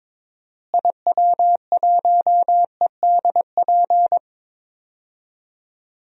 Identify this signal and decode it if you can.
Morse code, heard as a keyed tone, about 22 wpm, 700 Hz. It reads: IW1EDP
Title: DXCC-100 Call Signs - 6 Characters - Repeated 3x - 22wpm